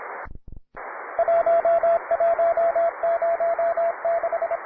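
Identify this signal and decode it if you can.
Morse code, heard as a keyed tone, about 27 wpm, 670 Hz, with radio noise. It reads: 1106
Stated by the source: Morse code QTC list traffic from the Worked All Europe (WAE) contest, advanced level.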